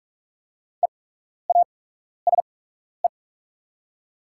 Morse code, heard as a keyed tone, about 44 wpm, spaced as for 12 wpm, 690 Hz. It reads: EASE
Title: 8x Character Spacing - Top 500 Words - 45wpm